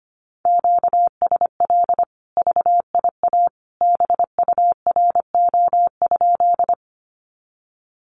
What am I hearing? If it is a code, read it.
Morse code, heard as a keyed tone, about 25 wpm, 700 Hz. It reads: QSL VIA BURO?
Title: Common QSO Phrases 25wpm